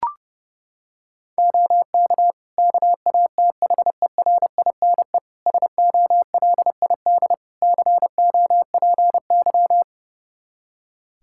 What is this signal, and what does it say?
Morse code, heard as a keyed tone, about 30 wpm, 700 Hz. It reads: OK KATHERINE SOLID COPY
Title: Practice QSO - Exchange 3 - More Info - 30wpm